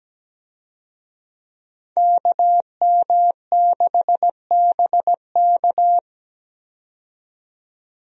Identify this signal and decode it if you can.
Morse code, heard as a keyed tone, about 17 wpm, 700 Hz. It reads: KM6BK
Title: USA Call Signs - 5 Characters - 17wpm